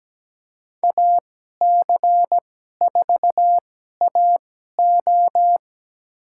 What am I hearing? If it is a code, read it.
Morse code, heard as a keyed tone, about 17 wpm, spaced as for 12 wpm, 700 Hz. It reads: AC4AO
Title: USA Call Signs - 5 Characters - 2x Character Spacing - 17wpm